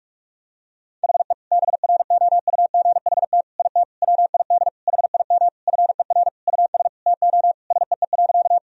Morse code, heard as a keyed tone, about 46 wpm, 700 Hz. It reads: HE BROUGHT IT WID HIM FER US TO SEE.
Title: Sentences from Top 500 Words and Abbreviations 45wpm